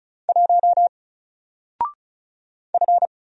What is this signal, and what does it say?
Morse code, heard as a keyed tone, about 35 wpm, 700 Hz. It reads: 1 F
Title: Single Letter-Number Alt 35wpm